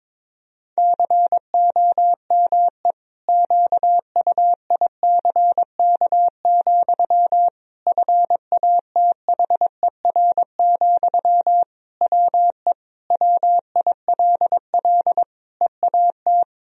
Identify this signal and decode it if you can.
Morse code, heard as a keyed tone, about 22 wpm, 700 Hz. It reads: COME QUICK, FATHER, WE WILL EAT
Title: Sentences from Top 1000 Words - Review Part 3 - 22wpm